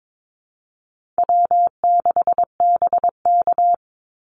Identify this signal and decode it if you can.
Morse code, heard as a keyed tone, about 22 wpm, 700 Hz. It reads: W6BK
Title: Easy Call Signs - 22wpm